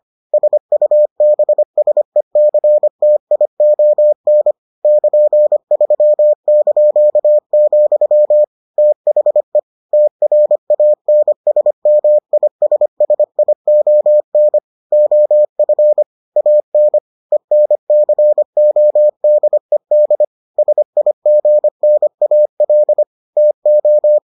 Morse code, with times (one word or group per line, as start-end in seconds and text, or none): 0.3-4.5 SUBSECTION
4.8-8.4 <KN>3),
8.8-9.6 THE
9.9-14.6 TRANSMISSION
14.9-16.0 OF
16.4-17.0 AN
17.3-20.2 ENCODED
20.6-23.0 SIGNAL
23.4-24.2 TO